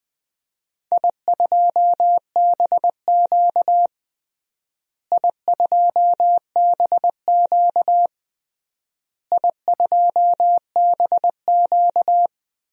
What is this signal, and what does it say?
Morse code, heard as a keyed tone, about 20 wpm, 700 Hz. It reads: I2BQ I2BQ I2BQ
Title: DXCC-100 Call Signs - 4 Characters - Repeated 3x - 20wpm